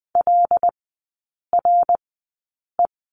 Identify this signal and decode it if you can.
Morse code, heard as a keyed tone, about 20 wpm, spaced as for 8 wpm, 700 Hz. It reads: LRE
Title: 074 Lesson 13e - Letter L - Three Characters 20wpm